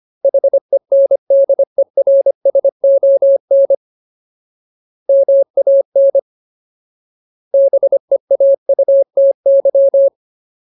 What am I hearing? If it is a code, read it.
Morse code, heard as a keyed tone, about 25 wpm, 550 Hz. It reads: HENDERSON MAN BEAUTY
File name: Ultimate Word List / 25WPM Ultimate